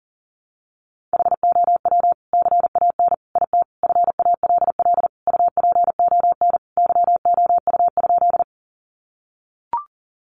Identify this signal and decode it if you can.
Morse code, heard as a keyed tone, about 40 wpm, 700 Hz. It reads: HOW CAN IT FALL UPON YOU?
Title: Sentences from Top 1100 Words 40wpm